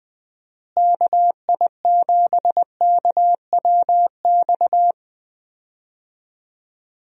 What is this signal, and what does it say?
Morse code, heard as a keyed tone, about 20 wpm, 700 Hz. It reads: KI7KWX
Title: USA Call Signs - 6 Characters - Repeated 3x - 20wpm